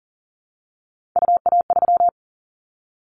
Morse code, heard as a keyed tone, about 38 wpm, 700 Hz. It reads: UA3
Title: Callsign Prefixes for Top 25 DXCC - 40wpm